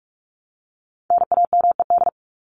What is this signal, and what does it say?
Morse code, heard as a keyed tone, about 44 wpm, 700 Hz. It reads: NAMED
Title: ED Suffix - Top 100 Words with Base Words 45wpm